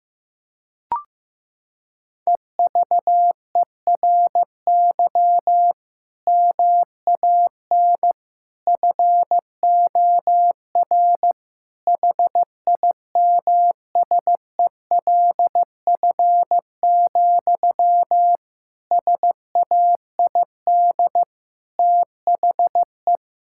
Morse code, heard as a keyed tone, about 15 wpm, 700 Hz. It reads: EVERY MAN FOR HIMSELF, SAID THE
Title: Sentences from Top 1600 Words 15wpm